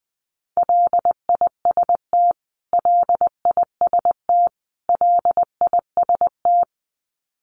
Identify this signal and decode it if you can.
Morse code, heard as a keyed tone, about 20 wpm, 700 Hz. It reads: LIST LIST LIST
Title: Top 500 Words Repeated 20wpm